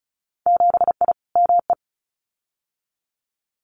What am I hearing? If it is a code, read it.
Morse code, heard as a keyed tone, about 34 wpm, 700 Hz. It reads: 7I ME
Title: Winter Field Day Exchanges Encore 35wpm